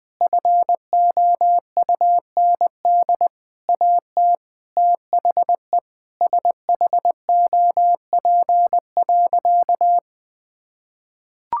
Morse code, heard as a keyed tone, about 21 wpm, 700 Hz. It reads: FOUND AT THE SHOP.
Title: Sentences from Top 1600 Words 20wpm